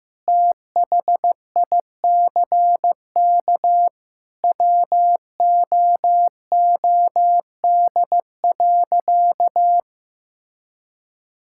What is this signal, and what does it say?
Morse code, heard as a keyed tone, about 15 wpm, 700 Hz. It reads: THICK WOOD.
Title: Sentences from Top 1600 Words 15wpm